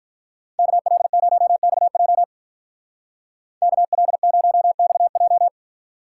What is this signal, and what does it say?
Morse code, heard as a keyed tone, about 43 wpm, 700 Hz. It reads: KL0XJ KL0XJ
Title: Speed Racing - USA 2x2 Call Signs 35wpm